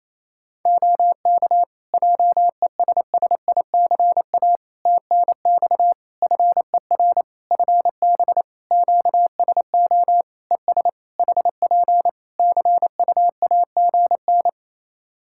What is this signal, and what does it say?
Morse code, heard as a keyed tone, about 28 wpm, 700 Hz. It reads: OK JESSICA TNX FER FB QSO ES HP CUAGN
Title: Practice QSO - Exchange 4 - Ending - 28wpm